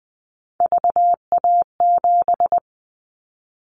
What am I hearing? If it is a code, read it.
Morse code, heard as a keyed tone, about 19 wpm, 700 Hz. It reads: VA7
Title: Callsign Prefixes for Top 50 DXCC - 20wpm